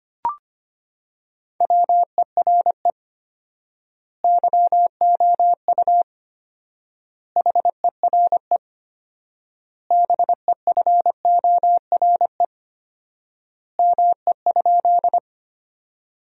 Morse code, heard as a keyed tone, about 25 wpm, 700 Hz. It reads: WERE YOU HERE BEFORE ME?
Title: 4x Word-Spacing - Sentences from Top 200 Words - 25wpm